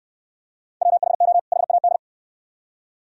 Morse code, heard as a keyed tone, about 68 wpm, 690 Hz. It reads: WHO HAD